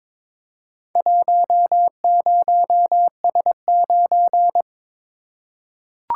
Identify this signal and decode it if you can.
Morse code, heard as a keyed tone, about 23 wpm, 700 Hz. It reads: 10S9
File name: practice / 22wpm / Four Letter-Number Combinations - 22wpm